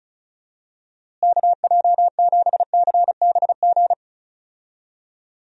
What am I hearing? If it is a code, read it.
Morse code, heard as a keyed tone, about 34 wpm, 700 Hz. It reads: KJ7CBG
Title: USA Call Signs - 6 Characters - Repeated 2x - 35wpm